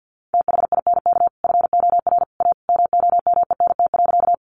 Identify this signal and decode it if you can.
Morse code, heard as a keyed tone, about 50 wpm, 700 Hz. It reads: THINK FOR A MOMENT.
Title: Sentences from Top 1000 Words 50wpm